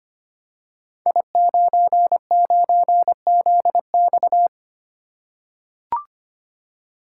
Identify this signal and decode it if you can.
Morse code, heard as a keyed tone, about 25 wpm, 700 Hz. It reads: I99ZX
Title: Code Groups - 5 Letter:Numbers - 25wpm